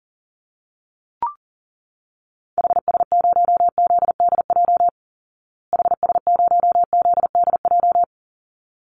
Morse code, heard as a keyed tone, about 40 wpm, 700 Hz. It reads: HS0ZDJ HS0ZDJ
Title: Call Signs Repeated 3x - 40wpm